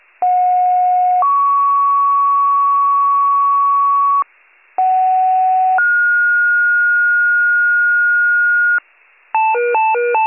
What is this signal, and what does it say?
Two-tone page, heard as a one-second tone, then a three-second tone, about 730 then 1120 Hz, about 750 then 1430 Hz, with noise. It is heard from 0.2 to 8.8 s.